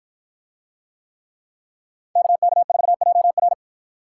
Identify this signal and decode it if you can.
Morse code, heard as a keyed tone, about 53 wpm, 700 Hz. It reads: KK4JR